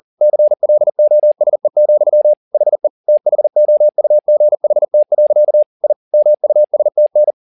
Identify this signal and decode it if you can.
Morse code, heard as a keyed tone, about 40 wpm, 600 Hz. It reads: CLOSE, HE THOUGHT. I MUSTN
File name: the old man and the sea 2 40wpm